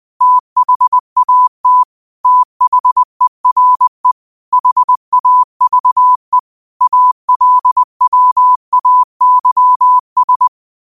Morse code, heard as a keyed tone, about 20 wpm, 1000 Hz. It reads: THAT THERE HAVE ALWAYS